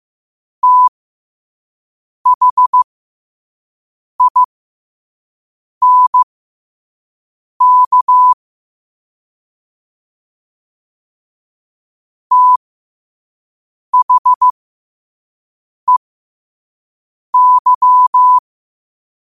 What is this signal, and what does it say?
Morse code, heard as a keyed tone, about 15 wpm, spaced as for 5 wpm, 1000 Hz. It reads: THINK THEY
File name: qotdfiles / 5wpm